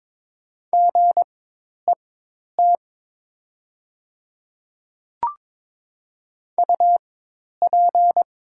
Morse code, heard as a keyed tone, about 22 wpm, spaced as for 10 wpm, 700 Hz. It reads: GET UP
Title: 4x Character Spacing - Top 200 Words - 22wpm